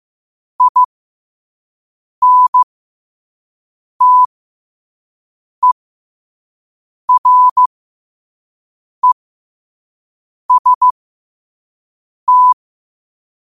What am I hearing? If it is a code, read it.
Morse code, heard as a keyed tone, about 15 wpm, spaced as for 5 wpm, 1000 Hz. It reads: INTEREST
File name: qotdfiles / 5wpm